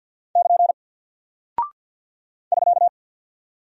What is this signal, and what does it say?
Morse code, heard as a keyed tone, about 48 wpm, 700 Hz. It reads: <KN> <SK>
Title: Single Character (Letter:Number:Punctuation:Prosign) Alt 50wpm